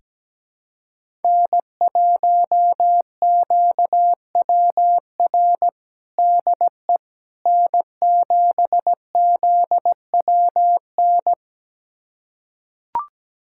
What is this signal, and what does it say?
Morse code, heard as a keyed tone, about 17 wpm, 700 Hz. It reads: N1QWR DE N7ZWN
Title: USA Callsign DE Callsign - 5 Characters - 17wpm